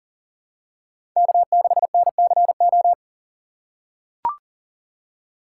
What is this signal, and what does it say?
Morse code, heard as a keyed tone, about 37 wpm, 700 Hz. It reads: K6NCO
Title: USA Call Signs - 5 Characters - 3x Character Spacing - 40wpm